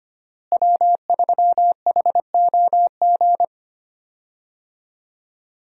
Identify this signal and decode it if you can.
Morse code, heard as a keyed tone, about 25 wpm, 700 Hz. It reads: W3HOG
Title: USA Call Signs - 5 Characters - Repeated 3x - 25wpm